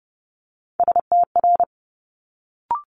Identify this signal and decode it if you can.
Morse code, heard as a keyed tone, about 30 wpm, 700 Hz. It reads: STR
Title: Multi-letter Phonemes Quick Fire 30wpm